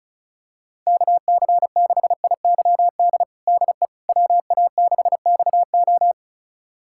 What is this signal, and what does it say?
Morse code, heard as a keyed tone, about 36 wpm, 700 Hz. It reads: KC6IYD DE WA6XO
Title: USA Callsign DE Callsign 35wpm